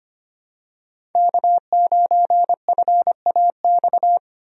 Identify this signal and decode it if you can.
Morse code, heard as a keyed tone, about 25 wpm, 700 Hz. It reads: K9FAX
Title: USA Call Signs - 5 Characters - Repeated 3x - 25wpm